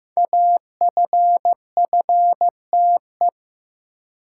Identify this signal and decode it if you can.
Morse code, heard as a keyed tone, about 15 wpm, 700 Hz. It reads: AFFTE
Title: Code Groups - 5 Letters - 15wpm